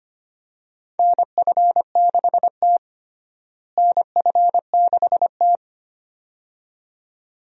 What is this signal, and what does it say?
Morse code, heard as a keyed tone, about 25 wpm, 700 Hz. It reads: NF6T NF6T
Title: USA Call Signs - 4 Characters - Repeated 2x - 25wpm